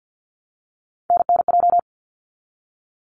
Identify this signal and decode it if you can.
Morse code, heard as a keyed tone, about 49 wpm, 700 Hz. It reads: NNJ
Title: ARRL Sections 50wpm